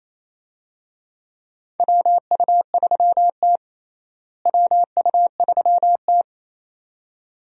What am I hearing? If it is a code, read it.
Morse code, heard as a keyed tone, about 28 wpm, 700 Hz. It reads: WU3T WU3T
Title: USA Call Signs - 4 Characters - Repeated 2x - 28wpm